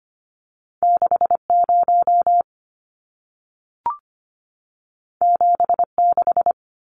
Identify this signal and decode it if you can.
Morse code, heard as a keyed tone, about 25 wpm, 700 Hz. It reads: 60 76
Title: Numbers - 2-Digit 25wpm